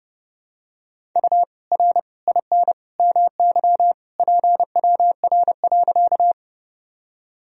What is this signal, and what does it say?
Morse code, heard as a keyed tone, about 30 wpm, 700 Hz. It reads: U R IN MY PWR.
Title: Sentences from Top 500 Words and Abbreviations 30wpm